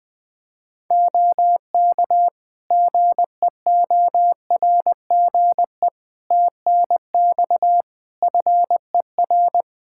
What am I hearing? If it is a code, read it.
Morse code, heard as a keyed tone, about 21 wpm, 700 Hz. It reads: OK GEORGE TNX FER
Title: K1USN SST QSO Practice 20wpm